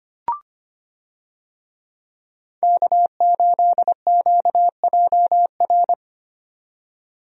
Morse code, heard as a keyed tone, about 25 wpm, 700 Hz. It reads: K8QJR
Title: USA Call Signs - 5 Characters - 25wpm